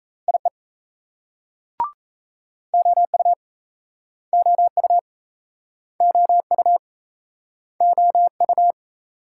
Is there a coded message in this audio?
Yes: Morse code from 0.3 to 8.7 s, heard as a keyed tone, about 36 wpm, 700 Hz. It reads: IE OU OU OU OU